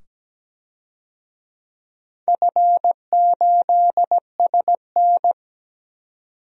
Morse code, heard as a keyed tone, about 17 wpm, 700 Hz. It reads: F8SN
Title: DXCC-100 Call Signs - 4 Characters - 17wpm